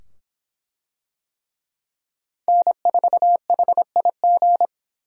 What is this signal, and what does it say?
Morse code, heard as a keyed tone, about 24 wpm, 700 Hz. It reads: N4HIG